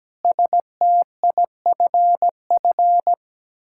Morse code, heard as a keyed tone, about 17 wpm, 700 Hz. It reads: STIFF
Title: Pangram Challenge v2 - 17wpm